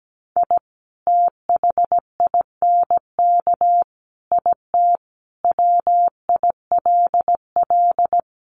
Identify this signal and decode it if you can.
Morse code, heard as a keyed tone, about 17 wpm, 700 Hz. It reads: I THINK IT WILL
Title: Sets of 5 Words Encore2 17wpm